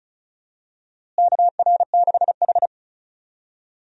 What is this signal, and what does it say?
Morse code, heard as a keyed tone, about 34 wpm, 700 Hz. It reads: KR6H
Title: USA Call Signs - 4 Characters - Repeated 3x - 35wpm